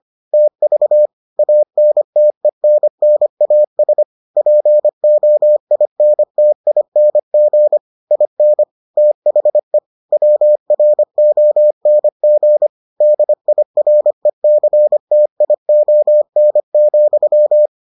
Morse code, heard as a keyed tone, about 25 wpm, 600 Hz. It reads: TV ANTENNAS POINTING IN THE WRONG DIRECTION,